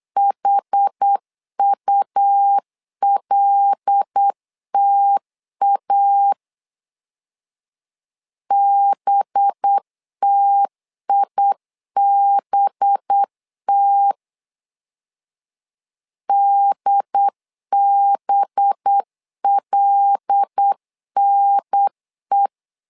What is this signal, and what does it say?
Morse code, heard as a keyed tone, about 8 wpm, 800 Hz. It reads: HULTA BTIBT DBLNE
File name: LEZIONI / 5 LBVC / 5 LB 8wpm 2816